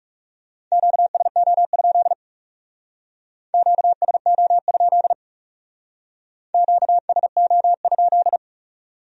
Morse code, heard as a keyed tone, about 38 wpm, 700 Hz. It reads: QSO? QSO? QSO?